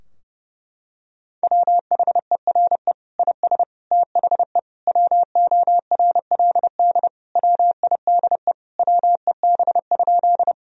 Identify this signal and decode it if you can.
Morse code, heard as a keyed tone, about 32 wpm, 700 Hz. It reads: WHERE IS THE WORLD WIDE WEB?
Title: Sentences from Top 1000 Words - Review Part 4 - 30wpm